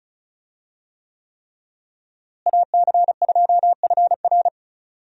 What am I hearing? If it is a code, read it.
Morse code, heard as a keyed tone, about 34 wpm, 700 Hz. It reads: AC2FR
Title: USA Call Signs - 5 Characters - Repeated 3x - 35wpm